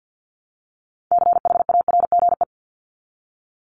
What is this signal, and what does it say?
Morse code, heard as a keyed tone, about 48 wpm, 700 Hz. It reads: CHARGE
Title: Top 751-1000 Words 50wpm